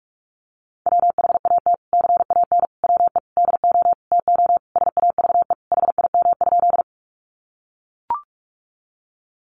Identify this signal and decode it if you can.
Morse code, heard as a keyed tone, about 44 wpm, 700 Hz. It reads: WHAT CAN WE DO TO SAVE HIM?
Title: Sentences from Top 1100 Words 45wpm